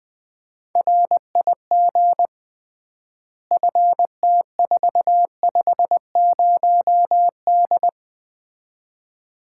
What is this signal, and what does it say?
Morse code, heard as a keyed tone, about 20 wpm, 700 Hz. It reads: RIG FT450D
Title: Common QSO Phrases - 3x Word-Spacing - 20wpm